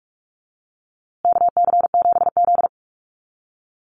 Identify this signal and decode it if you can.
Morse code, heard as a keyed tone, about 44 wpm, 700 Hz. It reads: KC7Z
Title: USA Call Signs Repeated 3x 45wpm